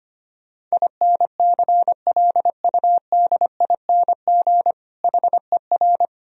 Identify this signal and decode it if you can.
Morse code, heard as a keyed tone, about 25 wpm, 700 Hz. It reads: INCLUDING HER
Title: Sentences from Top 1000 Words - Review Part 4 - 25wpm